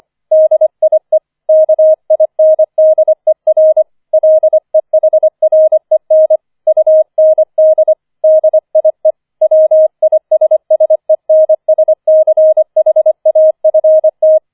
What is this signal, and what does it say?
Morse code, heard as a keyed tone, about 25 wpm, 630 Hz. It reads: DIE KINDER LEHREN UND DIE WISSENSCHAFT
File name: 004 Vierter Streich 25wpm